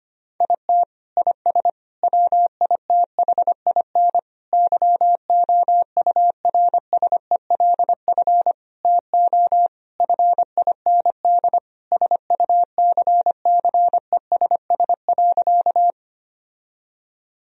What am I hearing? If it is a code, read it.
Morse code, heard as a keyed tone, about 25 wpm, 700 Hz. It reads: IT IS WITHIN YOURSELF TO FIND SUCCESS.
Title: Sentences from Top 1000 Words - Review Part 4 - 25wpm